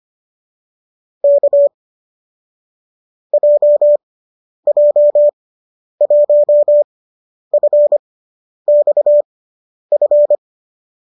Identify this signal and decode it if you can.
Morse code, heard as a keyed tone, about 25 wpm, spaced as for 10 wpm, 590 Hz. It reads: K JJ1FXF
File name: qso005 10